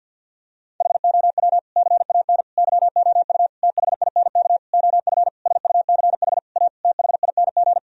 Morse code, heard as a keyed tone, about 50 wpm, 700 Hz. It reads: HOW CAN YOU THINK OF SUCH A THING
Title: Sentences from Top 1000 Words - Review Part 3 - 50wpm